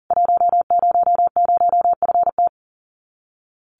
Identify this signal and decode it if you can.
Morse code, heard as a keyed tone, about 40 wpm, 700 Hz. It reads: 100FT
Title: Common QSO Phrases 40wpm